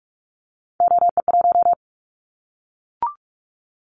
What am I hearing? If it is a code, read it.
Morse code, heard as a keyed tone, about 46 wpm, 700 Hz. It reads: OE1
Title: Callsign Prefixes for Top 100 DXCC - 45wpm